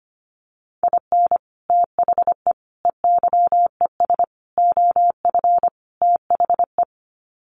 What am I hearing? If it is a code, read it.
Morse code, heard as a keyed tone, about 25 wpm, 700 Hz. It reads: IN THE EYES OF THE
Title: Sets of 5 Words 25wpm